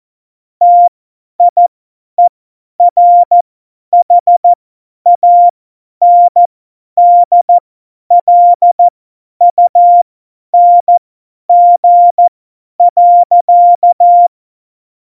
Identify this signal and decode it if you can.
Morse code, heard as a keyed tone, about 14 wpm, spaced as for 10 wpm, 700 Hz. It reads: TIERHANDLUNG.
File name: Morsekurs / Witze / Witz 22. Januar 2019, 50 F70